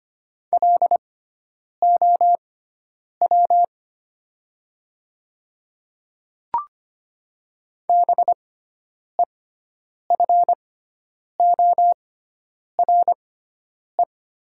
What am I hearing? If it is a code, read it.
Morse code, heard as a keyed tone, about 25 wpm, spaced as for 9 wpm, 700 Hz. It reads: LOW BEFORE
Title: 6x Character Spacing - Top 200 Words - 25wpm